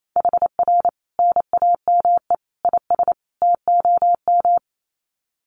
Morse code, heard as a keyed tone, about 28 wpm, 700 Hz. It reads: HR NAME IS TOM
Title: Common QSO Phrases 28wpm